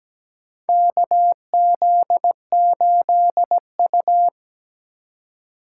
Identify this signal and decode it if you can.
Morse code, heard as a keyed tone, about 17 wpm, 700 Hz. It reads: KZ8U
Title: USA Call Signs - 4 Characters - 3x Character Spacing - 17wpm